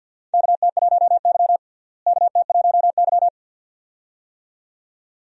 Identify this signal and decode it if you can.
Morse code, heard as a keyed tone, about 50 wpm, 700 Hz. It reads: KT1Y KT1Y
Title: USA Call Signs - 4 Characters - Repeated 2x - 50wpm